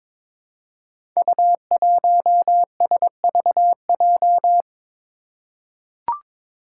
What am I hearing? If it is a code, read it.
Morse code, heard as a keyed tone, about 22 wpm, 700 Hz. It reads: U1SVJ
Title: Five Challenging Letter-Number Combinations 22wpm